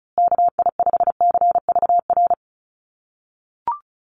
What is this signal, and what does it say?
Morse code, heard as a keyed tone, about 34 wpm, 700 Hz. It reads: KI5CVR
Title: USA Call Signs Repeated 2x 35wpm